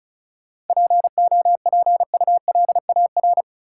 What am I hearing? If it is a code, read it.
Morse code, heard as a keyed tone, about 36 wpm, 700 Hz. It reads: POPULAR
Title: Sentences from Top 1000 Words - Review Part 3 - New Word Drill - 35wpm